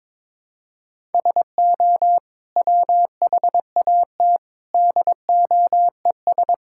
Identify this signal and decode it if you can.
Morse code, heard as a keyed tone, about 22 wpm, 700 Hz. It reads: SO WHAT DOES
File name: practice / 22wpm / Sets of 8 Words - 22wpm